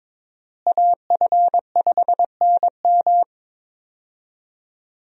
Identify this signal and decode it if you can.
Morse code, heard as a keyed tone, about 22 wpm, 700 Hz. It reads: AF5NM
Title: USA Call Signs - 5 Characters - Repeated 2x - 22wpm